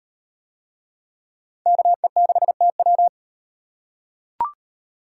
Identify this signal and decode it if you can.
Morse code, heard as a keyed tone, about 32 wpm, 700 Hz. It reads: KE6TW